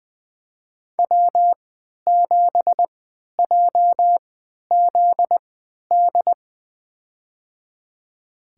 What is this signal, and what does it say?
Morse code, heard as a keyed tone, about 20 wpm, spaced as for 11 wpm, 700 Hz. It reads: W7JZD
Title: USA Call Signs - 5 Characters - 3x Character Spacing - 20wpm